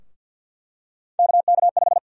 Morse code, heard as a keyed tone, about 51 wpm, 700 Hz. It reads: KK5
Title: USA Call Sign Prefixes - 50wpm